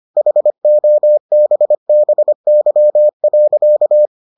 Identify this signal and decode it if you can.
Morse code, heard as a keyed tone, about 25 wpm, 600 Hz. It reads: HOBBY.